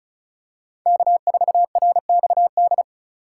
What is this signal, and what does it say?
Morse code, heard as a keyed tone, about 36 wpm, 700 Hz. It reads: K4RXD
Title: USA Call Signs - 4x Character Spacing - 35wpm